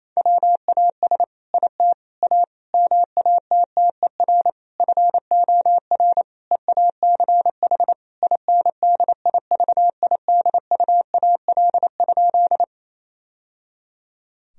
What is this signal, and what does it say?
Morse code, heard as a keyed tone, about 28 wpm, 700 Hz. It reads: WAS IT A MATTER FOR EACH INDIVIDUAL?
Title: Sentences from Top 1000 Words - Review Part 4 - 28wpm